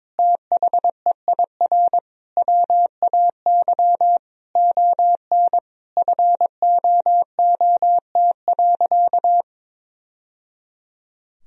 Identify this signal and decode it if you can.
Morse code, heard as a keyed tone, about 22 wpm, 700 Hz. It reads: THEIR WAY ON FOOT.
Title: Sentences from Top 1000 Words - Review Part 3 - 22wpm